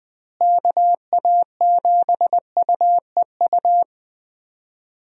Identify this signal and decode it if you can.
Morse code, heard as a keyed tone, about 20 wpm, 700 Hz. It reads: KA7UEU
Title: USA Call Signs - 6 Characters - Repeated 2x - 20wpm